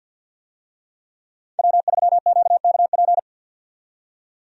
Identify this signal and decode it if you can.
Morse code, heard as a keyed tone, about 46 wpm, 700 Hz. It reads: W3QKP
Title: USA Call Signs - 5 Characters - 50wpm